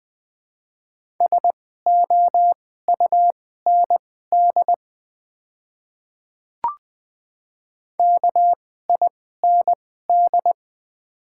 Morse code, heard as a keyed tone, about 20 wpm, spaced as for 14 wpm, 700 Hz. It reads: SOUND KIND
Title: 2x Character Spacing - Top 200 Words - 20wpm